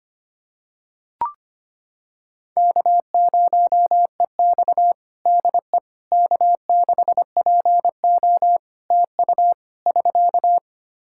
Morse code, heard as a keyed tone, about 25 wpm, 700 Hz. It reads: K0EX DE K6PO TU <SK>
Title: Practice QSO - Exchange 4 - Ending - 25wpm